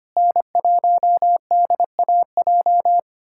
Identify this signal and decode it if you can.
Morse code, heard as a keyed tone, about 25 wpm, 700 Hz. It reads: N1DAJ
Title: USA Callsign DE Callsign 25wpm